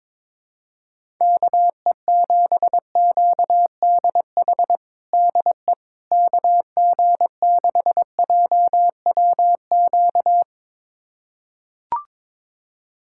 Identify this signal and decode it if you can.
Morse code, heard as a keyed tone, about 22 wpm, 700 Hz. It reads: KE7QDH DE KG6JWQ
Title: USA Callsign DE Callsign - 6 Characters - 22wpm